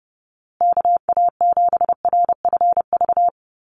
Morse code, heard as a keyed tone, about 30 wpm, 700 Hz. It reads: KA7RFV
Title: USA Call Signs Repeated 2x 30wpm